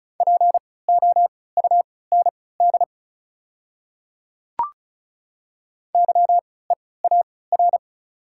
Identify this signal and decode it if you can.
Morse code, heard as a keyed tone, about 34 wpm, spaced as for 20 wpm, 700 Hz. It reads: POUND YEAR
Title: 3x Character Spacing - Top 500 Words - 35wpm